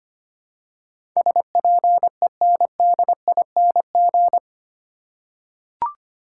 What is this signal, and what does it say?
Morse code, heard as a keyed tone, about 25 wpm, 700 Hz. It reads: SPENDING
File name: Eight Letter Words - 25wpm